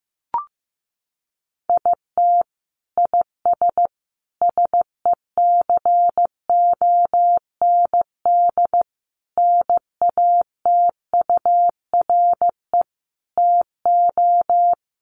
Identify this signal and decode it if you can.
Morse code, heard as a keyed tone, about 15 wpm, 700 Hz. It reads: IT IS SECOND NATURE TO
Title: Sentences from Top 600 Words 15wpm